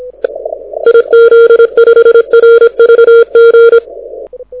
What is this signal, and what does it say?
Morse code, heard as a keyed tone, about 26 wpm, 3250 Hz, with noise. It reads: IZ5RVG
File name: audio zawody / IARU 2015 CW SN7Q